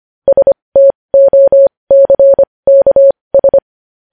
Morse code, heard as a keyed tone, about 25 wpm, spaced as for 20 wpm, 550 Hz. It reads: STOCKS
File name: podcast / News headlines-1x-25x20